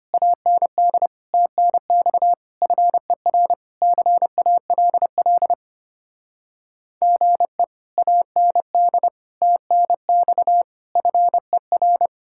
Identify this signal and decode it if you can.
Morse code, heard as a keyed tone, about 33 wpm, 700 Hz. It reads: AND TNX FER CALL GE AND TNX FER